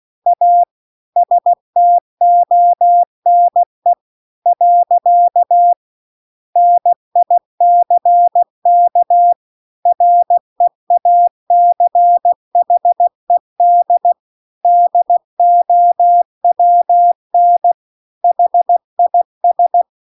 Morse code, heard as a keyed tone, about 16 wpm, 700 Hz. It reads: A STONE . NICK REACHED DOWN HIS
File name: CW-The-Big-Twoheaded-River-2-16WPM